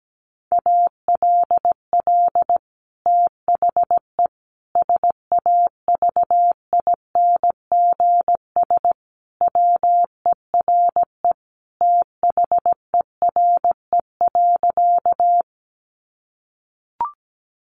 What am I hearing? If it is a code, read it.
Morse code, heard as a keyed tone, about 17 wpm, 700 Hz. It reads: ALL THE SAVINGS WERE THERE.
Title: Sentences from Top 1200 Words 17wpm